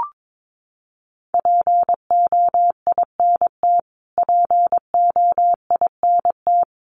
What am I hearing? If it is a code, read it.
Morse code, heard as a keyed tone, about 22 wpm, 700 Hz. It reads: POINT POINT
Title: Top 200 Words Repeated 22wpm